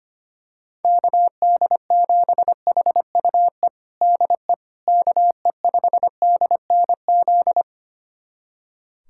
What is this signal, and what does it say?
Morse code, heard as a keyed tone, about 25 wpm, 700 Hz. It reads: KD7HUE DE KE5DNZ
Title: USA Callsign DE Callsign - 6 Characters - 25wpm